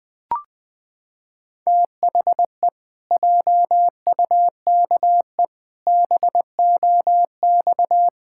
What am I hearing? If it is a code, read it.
Morse code, heard as a keyed tone, about 19 wpm, 700 Hz. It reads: THE JUKE BOX
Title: Pangram Challenge v2 - 20wpm